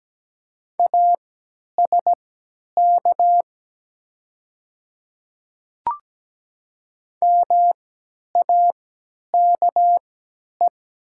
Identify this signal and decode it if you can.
Morse code, heard as a keyed tone, about 17 wpm, spaced as for 10 wpm, 700 Hz. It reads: ASK MAKE
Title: 3x Character Spacing - Top 300 Words - 17wpm